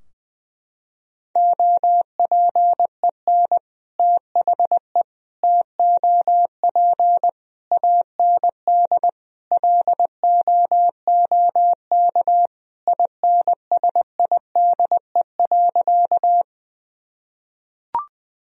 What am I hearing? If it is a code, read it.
Morse code, heard as a keyed tone, about 21 wpm, 700 Hz. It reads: OPEN THE TOP AND LOOK INSIDE.
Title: Sentences from Top 1000 Words - Review Part 2 - 20wpm